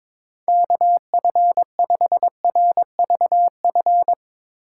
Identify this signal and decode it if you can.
Morse code, heard as a keyed tone, about 22 wpm, 700 Hz. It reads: KF5RVF
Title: USA Call Signs - 6 Characters - 4x Character Spacing - 22wpm